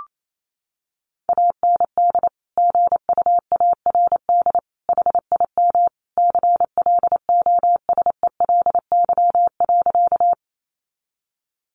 Morse code, heard as a keyed tone, about 28 wpm, 700 Hz. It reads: AND GUARD HIM CLOSELY.
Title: Sentences from Top 1400 Words 28wpm